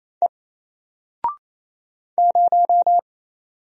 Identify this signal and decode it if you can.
Morse code, heard as a keyed tone, about 28 wpm, 700 Hz. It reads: E 0
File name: Single-Letter-Number-28wpm